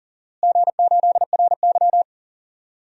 Morse code, heard as a keyed tone, about 40 wpm, 700 Hz. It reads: G8RY
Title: DXCC-100 Call Signs - 4 Characters - Repeated 3x - 40wpm